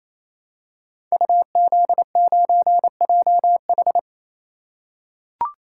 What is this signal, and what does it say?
Morse code, heard as a keyed tone, about 27 wpm, 700 Hz. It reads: UZ9JH
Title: Code Groups - 5 Letter:Numbers - 28wpm